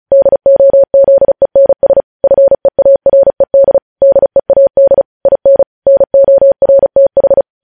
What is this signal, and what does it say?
Morse code, heard as a keyed tone, about 36 wpm, 550 Hz. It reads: DOZENS FEARED DEAD IN NORTH
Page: News Headlines in Morse Code at 35 WPM